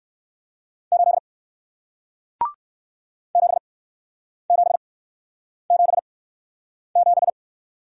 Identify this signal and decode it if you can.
Morse code, heard as a keyed tone, about 58 wpm, spaced as for 9 wpm, 700 Hz. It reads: 8 7777